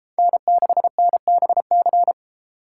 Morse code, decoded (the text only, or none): N6NBC